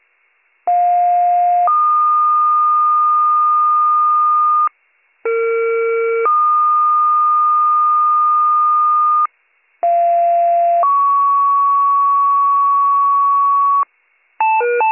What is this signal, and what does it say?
Two-tone page, heard as a one-second tone, then a three-second tone, about 710 then 1220 Hz, about 460 then 1220 Hz, about 690 then 1120 Hz. It is heard from 0.7 to 13.8 s.